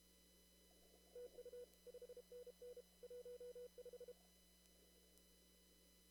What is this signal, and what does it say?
Morse code, heard as a keyed tone, about 32 wpm, 490 Hz. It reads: TU 5NN 15
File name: CW / kt5j wpx cw-op-k5pi